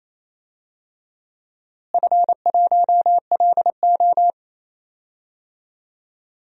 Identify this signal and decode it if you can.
Morse code, heard as a keyed tone, about 28 wpm, 700 Hz. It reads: F1LO
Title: DXCC-100 Call Signs - 4 Characters - Repeated 2x - 28wpm